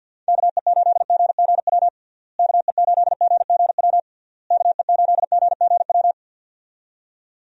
Morse code, heard as a keyed tone, about 48 wpm, 700 Hz. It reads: KE8GGW KE8GGW KE8GGW
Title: USA Call Signs - 6 Characters - Repeated 3x - 50wpm